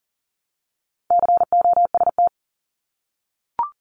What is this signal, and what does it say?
Morse code, heard as a keyed tone, about 41 wpm, 700 Hz. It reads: COST
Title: Multi-letter Phonemes 40wpm